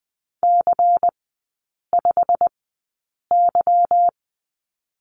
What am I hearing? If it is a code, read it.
Morse code, heard as a keyed tone, about 20 wpm, spaced as for 8 wpm, 700 Hz. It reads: C5Y
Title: 145 Lesson 24e - Letter y - Three Characters 20wpm